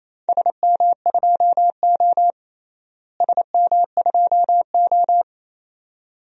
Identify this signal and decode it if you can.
Morse code, heard as a keyed tone, about 28 wpm, 700 Hz. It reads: SM2O SM2O
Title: DXCC-100 Call Signs - 4 Characters - Repeated 2x - 28wpm